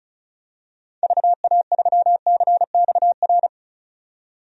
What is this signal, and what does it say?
Morse code, heard as a keyed tone, about 36 wpm, 700 Hz. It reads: VA3CXR
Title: DXCC-100 Call Signs - 6 Characters - 35wpm